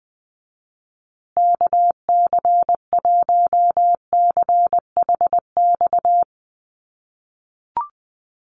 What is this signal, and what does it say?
Morse code, heard as a keyed tone, about 20 wpm, 700 Hz. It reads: KC1CHX
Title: USA Call Signs Repeated 3x 20wpm